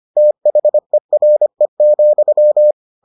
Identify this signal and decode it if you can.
Morse code, heard as a keyed tone, about 25 wpm, 600 Hz. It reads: THERE,